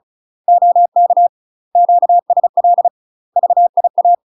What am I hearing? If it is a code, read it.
Morse code, heard as a keyed tone, about 35 wpm, 700 Hz. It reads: OK QSL VIA
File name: UK MASTER QSOs TRAINING / 35WPM / UK MASTER QSOs 0036